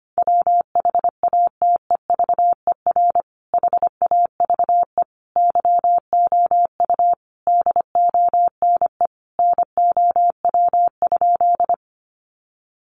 Morse code, heard as a keyed tone, about 25 wpm, 700 Hz. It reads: WHATEVER HAVE YOU DONE NOW?
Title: Sentences from Top 1100 Words 25wpm